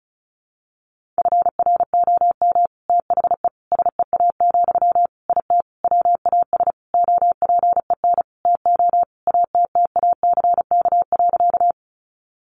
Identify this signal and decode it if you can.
Morse code, heard as a keyed tone, about 34 wpm, 700 Hz. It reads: FROM THE SEA, IT WAS OPEN TO ATTACK.
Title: Sentences from Top 1500 Words 35wpm